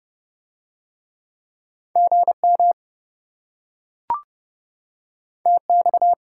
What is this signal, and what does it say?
Morse code, heard as a keyed tone, about 26 wpm, 700 Hz. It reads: GM TX